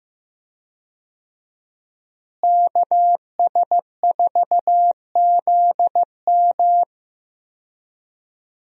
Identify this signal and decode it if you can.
Morse code, heard as a keyed tone, about 15 wpm, 700 Hz. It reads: KS4ZM
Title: USA Call Signs - 5 Characters - Repeated 2x - 15wpm